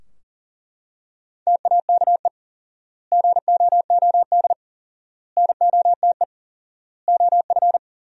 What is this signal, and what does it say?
Morse code, heard as a keyed tone, about 40 wpm, 700 Hz. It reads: TAKE GOOD NOTE OF
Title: 4x Word-Spacing - Sentences from Top 500 Words - 40wpm